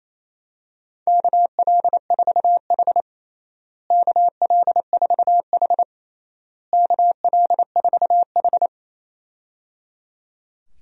Morse code, heard as a keyed tone, about 28 wpm, 700 Hz. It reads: KL4H KL4H KL4H
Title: USA Call Signs - 4 Characters - Repeated 3x - 28wpm